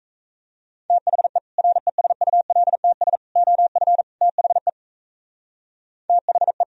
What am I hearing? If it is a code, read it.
Morse code, heard as a keyed tone, about 34 wpm, 700 Hz. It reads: THE RESULTS OF THE THE
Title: Speed-Racing - Sets of 4 Words 28wpm